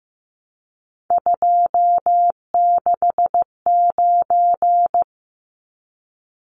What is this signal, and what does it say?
Morse code, heard as a keyed tone, about 15 wpm, 700 Hz. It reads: 269
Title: Numbers - 3-Digit 15wpm